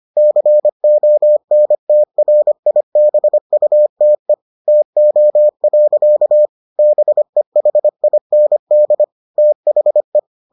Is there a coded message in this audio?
Yes: Morse code from 0.2 to 10.2 s, heard as a keyed tone, about 25 wpm, 600 Hz. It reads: CONTRIBUTE TO. BEHIND THE